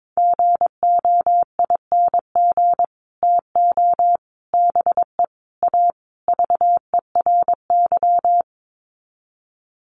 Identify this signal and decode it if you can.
Morse code, heard as a keyed tone, about 22 wpm, 700 Hz. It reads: GOING TO BE A VERY
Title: Sets of 5 Words Encore 22wpm